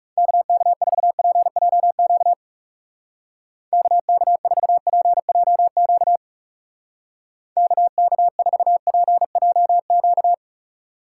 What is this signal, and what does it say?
Morse code, heard as a keyed tone, about 36 wpm, 700 Hz. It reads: KK4PJQ KK4PJQ KK4PJQ
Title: Speed Racing - USA 2x3 Call Signs 30wpm